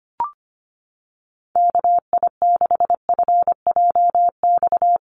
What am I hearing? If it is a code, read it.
Morse code, heard as a keyed tone, about 25 wpm, 700 Hz. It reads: KI6FJX
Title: USA Call Signs Repeated 3x 25wpm